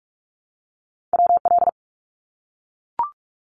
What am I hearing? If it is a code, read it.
Morse code, heard as a keyed tone, about 44 wpm, 700 Hz. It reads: WL
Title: Abbreviations 45wpm